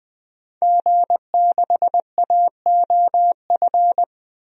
Morse code, heard as a keyed tone, about 20 wpm, 700 Hz. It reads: G6AOF
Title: DXCC-100 Call Signs - 5 Characters - 20wpm